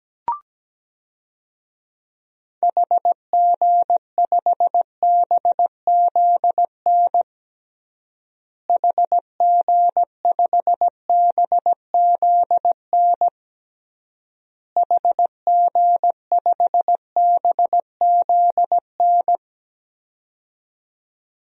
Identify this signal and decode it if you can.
Morse code, heard as a keyed tone, about 17 wpm, 700 Hz. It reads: HG5BZN HG5BZN HG5BZN
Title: DXCC-100 Call Signs - 6 Characters - Repeated 3x - 17wpm